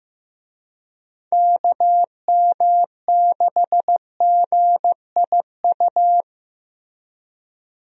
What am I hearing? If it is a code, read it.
Morse code, heard as a keyed tone, about 15 wpm, 700 Hz. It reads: KM6GIU
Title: USA Call Signs - 6 Characters - Repeated 2x - 15wpm